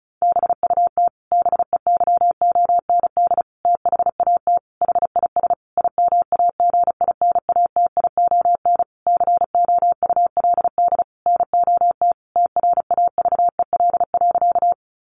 Morse code, heard as a keyed tone, about 34 wpm, 700 Hz. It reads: BUT BEYOND THAT HIS IMAGINATION COULD NOT TRAVEL.
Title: Sentences from Top 1300 Words 35wpm